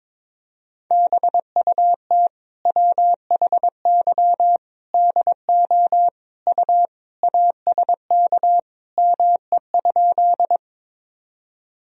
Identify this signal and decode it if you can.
Morse code, heard as a keyed tone, about 22 wpm, 700 Hz. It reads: BUT WHY DO U ASK ME?
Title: Sentences from Top 200 Words and Abbreviations 22wpm